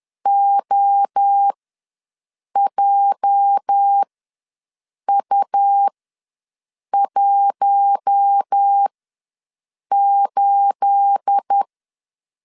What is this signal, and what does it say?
Morse code, heard as a keyed tone, about 10 wpm, spaced as for 6 wpm, 800 Hz. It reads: OJU18
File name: LEZIONI / 9 28 ws 10wpm 4124